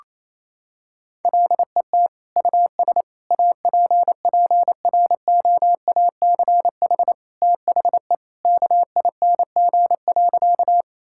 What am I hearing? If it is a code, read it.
Morse code, heard as a keyed tone, about 27 wpm, 700 Hz. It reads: LET US APPROACH THE KING.
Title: Sentences from Top 1000 Words - Review Part 6 - 28wpm